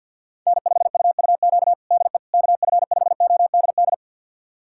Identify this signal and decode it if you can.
Morse code, heard as a keyed tone, about 50 wpm, 700 Hz. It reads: N5UUQ DE KF5ODD
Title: USA Callsign DE Callsign 50wpm